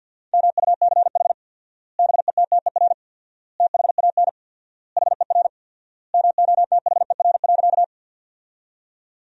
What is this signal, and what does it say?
Morse code, heard as a keyed tone, about 50 wpm, 700 Hz. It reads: MUCH BETTER THAN HER MOTHER.
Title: 4x Word-Spacing - Sentences from Top 400 Words - 50wpm